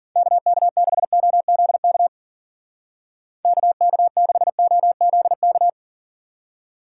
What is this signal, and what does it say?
Morse code, heard as a keyed tone, about 47 wpm, 700 Hz. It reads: KK6OZK KK6OZK
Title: Speed Racing - USA 2x3 Call Signs 40wpm